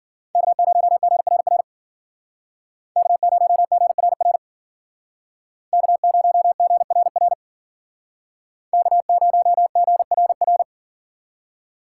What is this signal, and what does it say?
Morse code, heard as a keyed tone, about 51 wpm, 700 Hz. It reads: K0GRR K0GRR K0GRR K0GRR